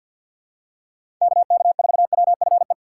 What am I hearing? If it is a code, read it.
Morse code, heard as a keyed tone, about 53 wpm, 700 Hz. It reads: KK4WFE
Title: USA Call Signs - 3x Character Spacing - 50wpm